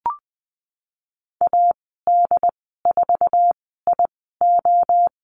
Morse code, heard as a keyed tone, about 20 wpm, spaced as for 14 wpm, 700 Hz. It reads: AD4IO
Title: 128 Lesson 21g - Number 6 - Call Signs 20wpm